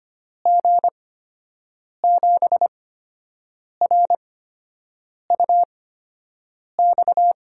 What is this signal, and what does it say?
Morse code, heard as a keyed tone, about 25 wpm, spaced as for 7 wpm, 700 Hz. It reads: G7RUX
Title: USA Call Signs - 6 Characters - 8x Character Spacing - 25wpm